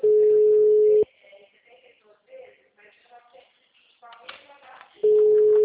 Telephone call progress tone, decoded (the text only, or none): ringback tone